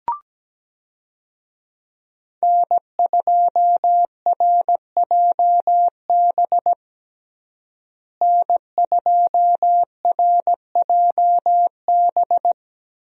Morse code, heard as a keyed tone, about 17 wpm, 700 Hz. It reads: N2RJB N2RJB
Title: USA Call Signs - 5 Characters - Repeated 2x - 17wpm